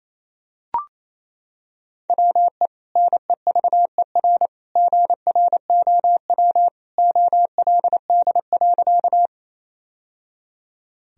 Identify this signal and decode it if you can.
Morse code, heard as a keyed tone, about 27 wpm, 700 Hz. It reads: WE NEVER GROW OLD.
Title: Sentences from Top 1000 Words - Review Part 1 - 28wpm